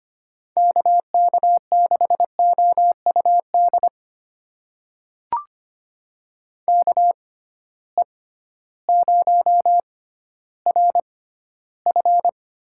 Morse code, heard as a keyed tone, about 25 wpm, spaced as for 13 wpm, 700 Hz. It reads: KK6OUD KE0RF
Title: USA Call Signs - 6 Characters - 6x Character Spacing - 25wpm